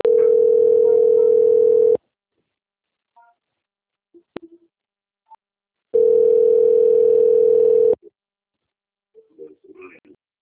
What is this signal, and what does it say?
Ringback tone (North America), 440 + 480 Hz.